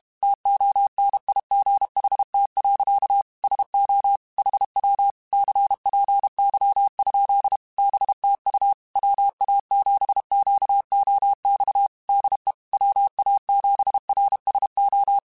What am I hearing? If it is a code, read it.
Morse code, heard as a keyed tone, about 32 wpm, 800 Hz. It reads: TONIGHT. SO HW CPY? BTU WA7QOX DE WA7RSO
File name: QSOs / QSOs-mp3s / WA7QOX 1@30